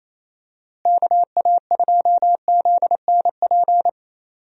Morse code, heard as a keyed tone, about 27 wpm, 700 Hz. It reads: KA2ZNP
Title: USA Call Signs - 6 Characters - Repeated 2x - 28wpm